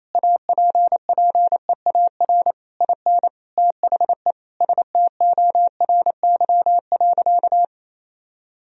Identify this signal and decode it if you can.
Morse code, heard as a keyed tone, about 27 wpm, 700 Hz. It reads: APPEAR IN THE STORY.
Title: Sentences from Top 1000 Words - Review Part 2 - 28wpm